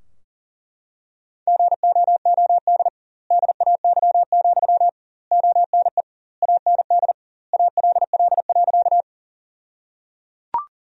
Morse code, heard as a keyed tone, about 37 wpm, 700 Hz. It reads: GOOD DAY, ONE AND ALL.
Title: 2x Word-Spacing - Sentences from Top 200 Words - 40wpm